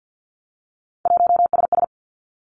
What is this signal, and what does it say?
Morse code, heard as a keyed tone, about 50 wpm, 700 Hz. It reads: 1SS
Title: Three Challenging Letter-Number Combinations 50wpm